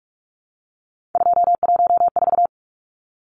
Morse code, heard as a keyed tone, about 44 wpm, 700 Hz. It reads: 214
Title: Numbers - 3-Digit 45wpm